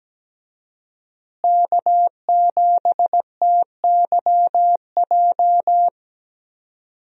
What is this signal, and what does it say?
Morse code, heard as a keyed tone, about 17 wpm, 700 Hz. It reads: K7TYJ
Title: USA Call Signs - 5 Characters - Repeated 2x - 17wpm